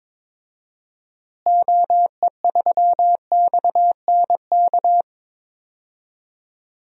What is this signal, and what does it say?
Morse code, heard as a keyed tone, about 22 wpm, 700 Hz. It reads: OE3XNK
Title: DXCC-100 Call Signs - 6 Characters - Repeated 3x - 22wpm